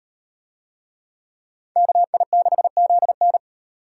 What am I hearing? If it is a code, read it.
Morse code, heard as a keyed tone, about 38 wpm, 700 Hz. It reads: KI6ZN